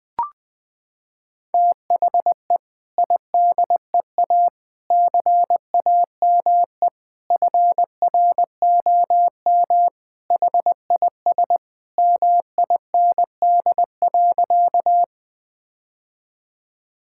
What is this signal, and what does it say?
Morse code, heard as a keyed tone, about 20 wpm, 700 Hz. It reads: THE IDEA CAME FROM HIS MIND.
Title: Sentences with 2-Word Phrases - 20wpm